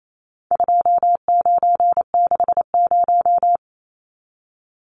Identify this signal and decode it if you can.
Morse code, heard as a keyed tone, about 28 wpm, 700 Hz. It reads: 2960
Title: Numbers - 4-Digit 28wpm